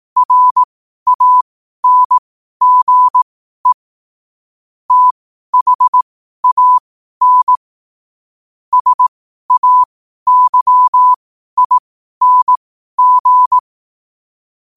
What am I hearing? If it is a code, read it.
Morse code, heard as a keyed tone, about 18 wpm, spaced as for 12 wpm, 1000 Hz. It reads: RANGE THAN SAYING
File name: 12wpm